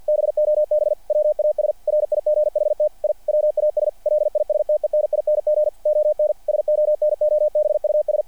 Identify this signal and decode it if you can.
Morse code, heard as a keyed tone, about 48 wpm, 600 Hz. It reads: BOB WAS RIGHT I WAS LISTENING ON SONOBUS
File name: archived / CRSnet